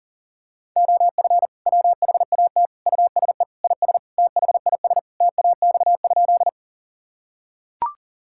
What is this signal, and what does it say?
Morse code, heard as a keyed tone, about 37 wpm, 700 Hz. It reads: OF WHAT USE IS THIS TAX?
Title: Sentences from Top 1000 Words - Review Part 5 - 40wpm